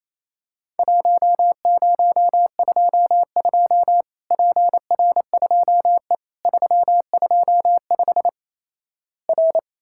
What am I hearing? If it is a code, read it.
Morse code, heard as a keyed tone, about 28 wpm, 700 Hz. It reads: 1022 PR2E 325 R
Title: Worked All Europe DX Contest - QTC Exchange - 28wpm